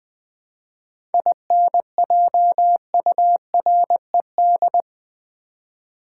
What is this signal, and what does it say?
Morse code, heard as a keyed tone, about 20 wpm, 700 Hz. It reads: INJURED
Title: 7 Letter Words - First 3 Letters Given - 20wpm